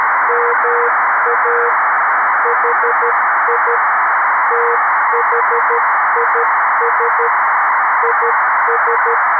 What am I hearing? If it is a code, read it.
Morse code, heard as a keyed tone, about 12 wpm, 480 Hz, with noise. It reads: MA HI THIS IS
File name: arissat / recordings / ARISSat 11Sept2011 1354utc 60frames